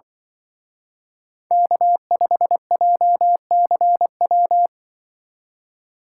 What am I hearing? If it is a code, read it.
Morse code, heard as a keyed tone, about 24 wpm, 700 Hz. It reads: K5JCW
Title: Speed Racing - USA 1x3 Call Signs 20wpm